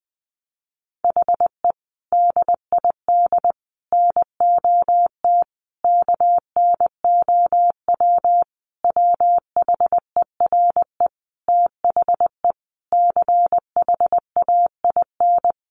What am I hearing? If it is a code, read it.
Morse code, heard as a keyed tone, about 20 wpm, 700 Hz. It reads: HE DID NOT KNOW WHERE THE CHAIN
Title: Sentences from Top 1500 Words 20wpm